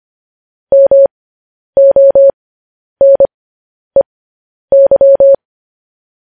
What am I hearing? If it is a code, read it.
Morse code, heard as a keyed tone, about 25 wpm, spaced as for 10 wpm, 550 Hz. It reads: MONEY
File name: podcast / News headlines-1x-25x10